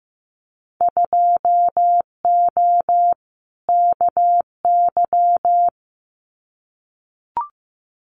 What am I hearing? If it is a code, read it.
Morse code, heard as a keyed tone, about 15 wpm, 700 Hz. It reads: 2O KY
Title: Winter Field Day Exchanges 15wpm